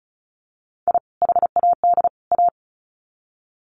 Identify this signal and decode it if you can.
Morse code, heard as a keyed tone, about 34 wpm, 700 Hz. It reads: I HAD A
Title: Sets of 3 Words 35wpm